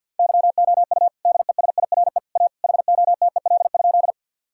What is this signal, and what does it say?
Morse code, heard as a keyed tone, about 50 wpm, 700 Hz. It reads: YOU DESIRE A HOTEL?
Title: Sentences from Top 1600 Words 50wpm